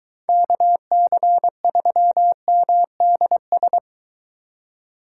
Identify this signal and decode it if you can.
Morse code, heard as a keyed tone, about 23 wpm, 700 Hz. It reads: KC3MDS